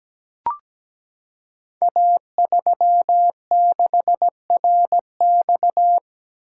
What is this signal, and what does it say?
Morse code, heard as a keyed tone, about 17 wpm, 700 Hz. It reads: A36RX
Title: Code Groups - 5 Letter:Numbers - 17wpm